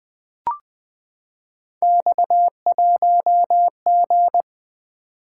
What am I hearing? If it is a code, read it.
Morse code, heard as a keyed tone, about 20 wpm, 700 Hz. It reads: X1G
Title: Three Letter-Number Combinations - 20wpm